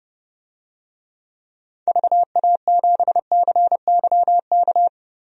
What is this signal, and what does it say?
Morse code, heard as a keyed tone, about 32 wpm, 700 Hz. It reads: VA7CYK
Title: DXCC-100 Call Signs - 6 Characters - 30wpm